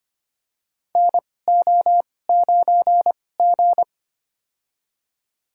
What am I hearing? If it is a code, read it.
Morse code, heard as a keyed tone, about 25 wpm, spaced as for 18 wpm, 700 Hz. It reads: NO9G
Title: USA Call Signs - 4 Characters - 2x Character Spacing - 25wpm